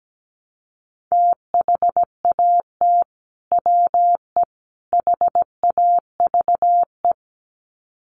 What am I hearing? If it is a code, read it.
Morse code, heard as a keyed tone, about 17 wpm, 700 Hz. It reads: THAT WE HAVE
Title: Sets of 3 Words 17wpm